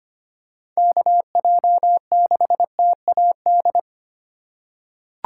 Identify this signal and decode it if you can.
Morse code, heard as a keyed tone, about 25 wpm, 700 Hz. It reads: KJ6TAD
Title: USA Call Signs - 3x Character Spacing - 25wpm